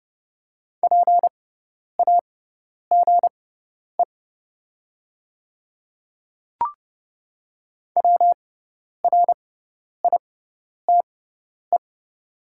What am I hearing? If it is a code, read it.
Morse code, heard as a keyed tone, about 30 wpm, spaced as for 10 wpm, 700 Hz. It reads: PAGE WRITE
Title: 6x Character Spacing - Top 300 Words - 30wpm